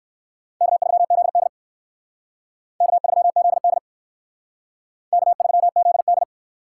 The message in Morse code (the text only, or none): K3ZD K3ZD K3ZD